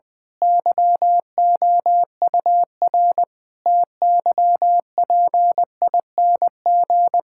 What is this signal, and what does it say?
Morse code, heard as a keyed tone, about 20 wpm, 700 Hz. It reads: YOUR TYPING
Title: Pangram Challenge v2 - 20wpm